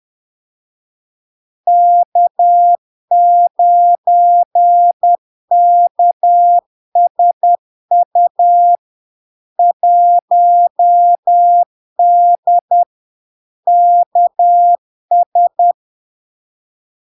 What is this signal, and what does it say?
Morse code, heard as a keyed tone, about 10 wpm, 700 Hz. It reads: K9KSU 1D KS
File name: LARK-FD-2022-Log-in-CW-10wpm-700-Hz